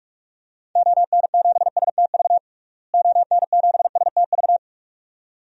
Morse code, heard as a keyed tone, about 46 wpm, 700 Hz. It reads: ON7STV ON7STV
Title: DXCC-100 Call Signs - 6 Characters - Repeated 2x - 45wpm